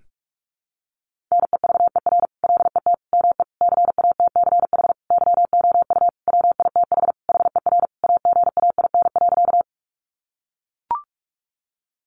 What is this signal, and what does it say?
Morse code, heard as a keyed tone, about 44 wpm, 700 Hz. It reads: NEVER LET ME CATCH YOU WITH HER AGAIN.
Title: Sentences from Top 1500 Words 45wpm